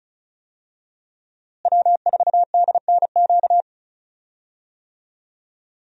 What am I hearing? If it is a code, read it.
Morse code, heard as a keyed tone, about 36 wpm, 700 Hz. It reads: W4DNQ
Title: USA Call Signs - 5 Characters - Repeated 3x - 35wpm